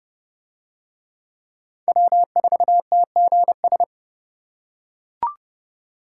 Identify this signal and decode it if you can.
Morse code, heard as a keyed tone, about 28 wpm, 700 Hz. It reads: W4TGS